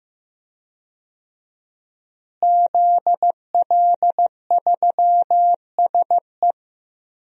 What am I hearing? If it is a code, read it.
Morse code, heard as a keyed tone, about 15 wpm, 700 Hz. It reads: ZL3SE
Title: DXCC-100 Call Signs - 5 Characters - Repeated 2x - 15wpm